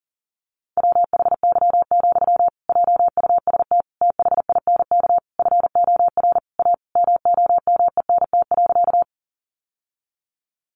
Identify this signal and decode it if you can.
Morse code, heard as a keyed tone, about 40 wpm, 700 Hz. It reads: WHY, JUST THINK FOR A MOMENT.
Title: Sentences from Top 1000 Words 40wpm